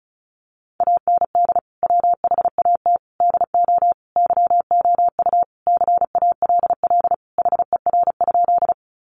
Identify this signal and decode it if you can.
Morse code, heard as a keyed tone, about 34 wpm, 700 Hz. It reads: AND WHAT DO YOU CALL HER?
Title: Sentences from Top 100 Words 35wpm